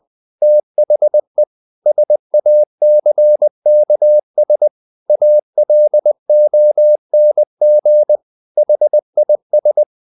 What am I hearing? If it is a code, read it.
Morse code, heard as a keyed tone, about 20 wpm, 600 Hz. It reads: THE SACKS ALONG HIS
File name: the old man and the sea / the old man and the sea part 1 20wpm